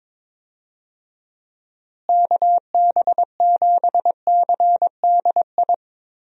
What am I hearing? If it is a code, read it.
Morse code, heard as a keyed tone, about 22 wpm, 700 Hz. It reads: KB7CDI
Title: USA Call Signs - 6 Characters - Repeated 3x - 22wpm